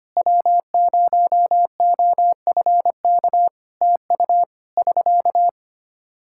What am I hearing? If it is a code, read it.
Morse code, heard as a keyed tone, about 25 wpm, 700 Hz. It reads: W0OFK TU <SK>
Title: K1USN SST QSO Practice 25wpm